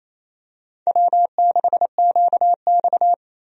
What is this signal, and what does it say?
Morse code, heard as a keyed tone, about 28 wpm, 690 Hz. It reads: W6QX
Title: USA Call Signs - 4 Characters - 28wpm